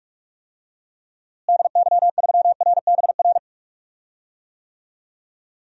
Morse code, heard as a keyed tone, about 46 wpm, 700 Hz. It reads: DY3RBR
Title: DXCC-100 Call Signs - 6 Characters - Repeated 2x - 45wpm